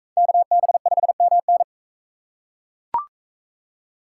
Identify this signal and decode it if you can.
Morse code, heard as a keyed tone, about 41 wpm, 700 Hz. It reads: KB5MN